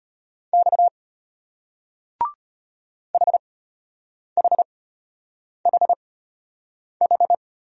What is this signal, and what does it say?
Morse code, heard as a keyed tone, about 33 wpm, spaced as for 8 wpm, 700 Hz. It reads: X HHHH